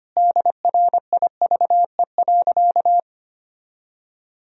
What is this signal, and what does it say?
Morse code, heard as a keyed tone, about 25 wpm, 700 Hz. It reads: DRIVE.
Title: Sentences from Top 1000 Words - Review Part 2 - 25wpm